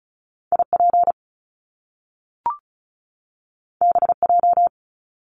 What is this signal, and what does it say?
Morse code, heard as a keyed tone, about 36 wpm, 700 Hz. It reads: IP BJ
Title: Top Two Letter Combinations in Words 35wpm